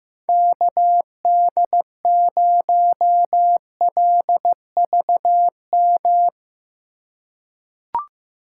Morse code, heard as a keyed tone, about 15 wpm, 700 Hz. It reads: KD0LVM
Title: USA Call Signs - 6 Characters - 3x Character Spacing - 15wpm